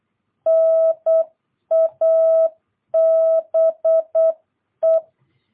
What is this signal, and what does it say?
Morse code, heard as a keyed tone, about 8 wpm, 650 Hz. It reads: NABE
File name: jh8cbh / 2022cw